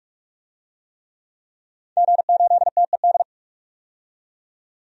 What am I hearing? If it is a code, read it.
Morse code, heard as a keyed tone, about 42 wpm, 700 Hz. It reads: G8TED
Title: DXCC-100 Call Signs - 5 Characters - 45wpm